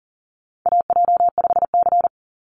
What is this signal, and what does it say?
Morse code, heard as a keyed tone, about 40 wpm, 700 Hz. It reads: AJ5C
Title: NA SOTA Chaser Callsigns 40wpm